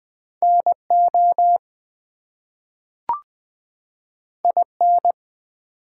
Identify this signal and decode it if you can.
Morse code, heard as a keyed tone, about 20 wpm, 700 Hz. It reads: NO IN
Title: Two Letter Words - 20wpm